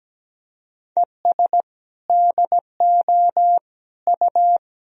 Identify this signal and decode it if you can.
Morse code, heard as a keyed tone, about 17 wpm, 700 Hz. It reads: ES DO U
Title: Sentences from Top 200 Words and Abbreviations 17wpm